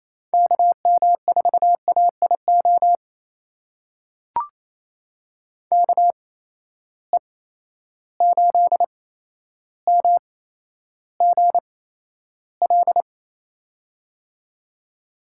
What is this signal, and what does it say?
Morse code, heard as a keyed tone, about 28 wpm, spaced as for 12 wpm, 700 Hz. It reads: KM4AIO KE8MGL
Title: USA Call Signs - 6 Characters - 8x Character Spacing - 28wpm